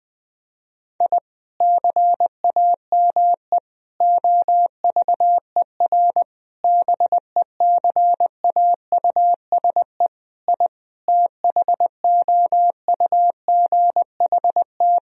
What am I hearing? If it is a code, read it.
Morse code, heard as a keyed tone, about 20 wpm, 700 Hz. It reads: I CAME OVER BECAUSE I THOUGHT
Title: Sentences from Top 1000 Words - Review Part 3 - 20wpm